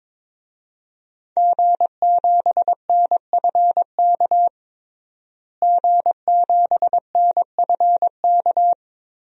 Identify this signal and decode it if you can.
Morse code, heard as a keyed tone, about 22 wpm, 700 Hz. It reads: G7NFK G7NFK
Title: DXCC-100 Call Signs - 5 Characters - Repeated 3x - 22wpm